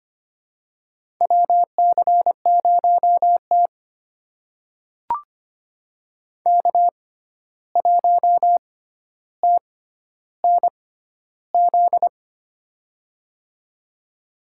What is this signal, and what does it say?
Morse code, heard as a keyed tone, about 25 wpm, spaced as for 12 wpm, 700 Hz. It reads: WC0T K1TNZ
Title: USA Call Signs - 6x Character Spacing - 25wpm